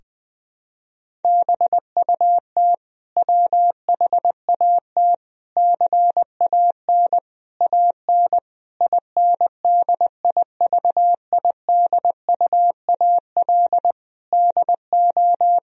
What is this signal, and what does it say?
Morse code, heard as a keyed tone, about 20 wpm, 700 Hz. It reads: BUT WHAT CAN AN INDIVIDUAL DO
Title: Sentences from Top 1000 Words - Review Part 4 - 20wpm